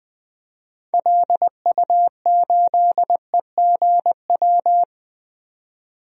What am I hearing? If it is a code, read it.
Morse code, heard as a keyed tone, about 20 wpm, 700 Hz. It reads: LU8EGW
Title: DXCC-100 Call Signs - 6 Characters - 20wpm